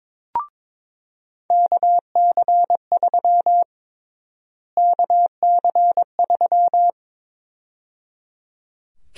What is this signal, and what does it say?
Morse code, heard as a keyed tone, about 22 wpm, 700 Hz. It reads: KC3 KC3
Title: USA Call Sign Prefixes - Repeated 2x - 22wpm